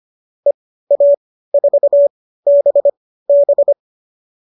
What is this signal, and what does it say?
Morse code, heard as a keyed tone, about 25 wpm, spaced as for 15 wpm, 570 Hz. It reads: EA4BB